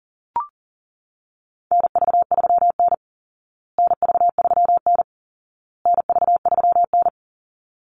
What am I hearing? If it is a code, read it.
Morse code, heard as a keyed tone, about 40 wpm, 700 Hz. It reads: NV3N NV3N NV3N
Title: Call Signs Repeated 2x - 40wpm